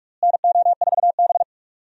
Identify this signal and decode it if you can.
Morse code, heard as a keyed tone, about 46 wpm, 700 Hz. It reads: NO4B
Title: USA Call Signs - 4 Characters - Repeated 2x - 45wpm